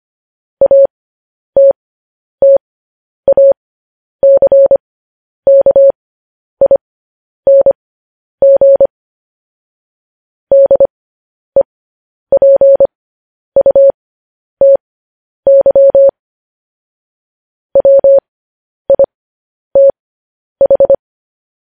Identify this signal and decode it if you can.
Morse code, heard as a keyed tone, about 25 wpm, spaced as for 10 wpm, 550 Hz. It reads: ATTACKING DEPUTY WITH
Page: News Headlines in Morse Code at 10 WPM